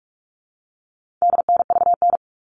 Morse code, heard as a keyed tone, about 44 wpm, 700 Hz. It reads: DNVN
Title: All Cut Numbers - 4-Digit 45wpm